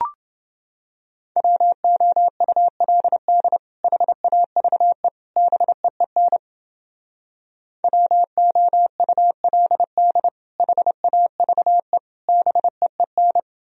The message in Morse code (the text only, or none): WOULD HAVE BEEN WOULD HAVE BEEN